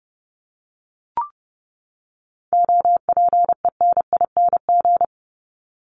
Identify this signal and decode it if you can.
Morse code, heard as a keyed tone, about 30 wpm, 700 Hz. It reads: OPENING
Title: ING Suffix - Top 100 Words with Base Words 30wpm